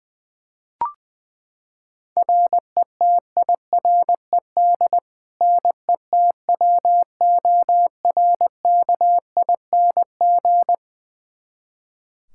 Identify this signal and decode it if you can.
Morse code, heard as a keyed tone, about 21 wpm, 700 Hz. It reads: RETIRED NETWORKING
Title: Practice QSO - Exchange 3 - More Info - 20wpm